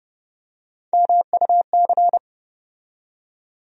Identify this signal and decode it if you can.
Morse code, heard as a keyed tone, about 30 wpm, 700 Hz. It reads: MUC
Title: USA Call Sign Postfixes - 3 Letters 30wpm